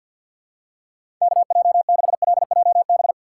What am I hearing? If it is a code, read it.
Morse code, heard as a keyed tone, about 47 wpm, 700 Hz. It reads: KJ6LJB
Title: USA Call Signs - 6 Characters - 4x Character Spacing - 50wpm